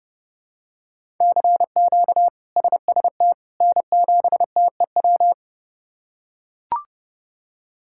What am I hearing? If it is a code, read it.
Morse code, heard as a keyed tone, about 30 wpm, 700 Hz. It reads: CQ SST N7TEW
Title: K1USN SST QSO Practice - 30wpm